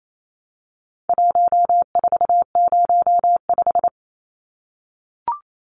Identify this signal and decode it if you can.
Morse code, heard as a keyed tone, about 28 wpm, 700 Hz. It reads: 1405
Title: Numbers - 4-Digit 28wpm